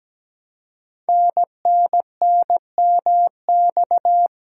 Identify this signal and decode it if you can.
Morse code, heard as a keyed tone, about 17 wpm, 700 Hz. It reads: NNNMX
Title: Code Groups - 5 Letters - 17wpm